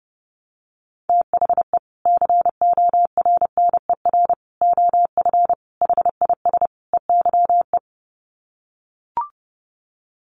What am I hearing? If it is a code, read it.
Morse code, heard as a keyed tone, about 28 wpm, 700 Hz. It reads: THE CORNER OF HIS EYE
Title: Sets of 5 Words Encore2 30wpm